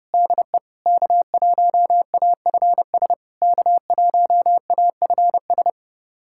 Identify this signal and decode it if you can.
Morse code, heard as a keyed tone, about 30 wpm, 690 Hz. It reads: DE K1AFS K1AFS
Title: SOTA QSO Practice 30wpm